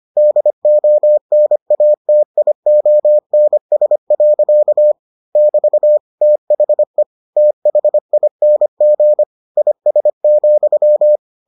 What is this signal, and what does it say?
Morse code, heard as a keyed tone, about 25 wpm, 600 Hz. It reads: DONATIONS. <BT> THE THING IS,